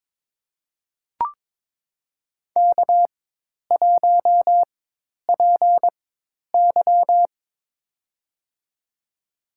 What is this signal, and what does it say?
Morse code, heard as a keyed tone, about 22 wpm, spaced as for 10 wpm, 700 Hz. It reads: K1PY
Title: USA Call Signs - 4 Characters - 4x Character Spacing - 22wpm